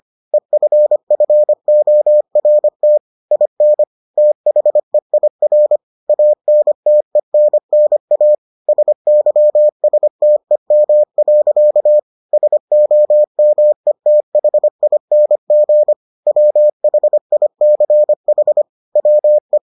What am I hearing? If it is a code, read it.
Morse code, heard as a keyed tone, about 25 wpm, 600 Hz. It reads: EFFORT IN THEIR ANTENNA SYSTEM. SOMETHING WHICH WE